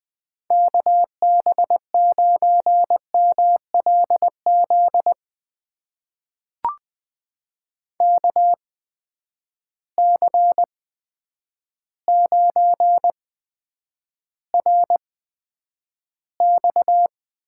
Morse code, heard as a keyed tone, about 20 wpm, spaced as for 9 wpm, 700 Hz. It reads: KB9MLZ KC9RX
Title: USA Call Signs - 8x Character Spacing - 20wpm